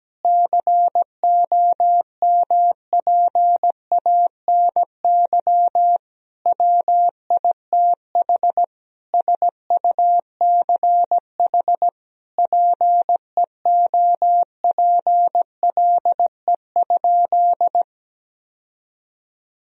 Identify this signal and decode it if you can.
Morse code, heard as a keyed tone, about 17 wpm, 700 Hz. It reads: COMPANY WITH SUCH PEOPLE?
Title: Sentences from Top 1000 Words - Review Part 3 - 17wpm